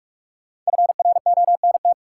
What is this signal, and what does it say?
Morse code, heard as a keyed tone, about 46 wpm, 700 Hz. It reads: FRONT
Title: Sentences from Top 500 Words - New Word Drill - 45wpm